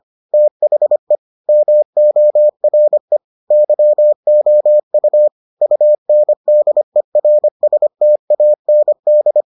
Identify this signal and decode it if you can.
Morse code, heard as a keyed tone, about 25 wpm, 600 Hz. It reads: THE MORE YOU UNDERSTAND